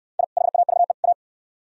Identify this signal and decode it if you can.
Morse code, heard as a keyed tone, about 69 wpm, 700 Hz. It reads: I HAVE A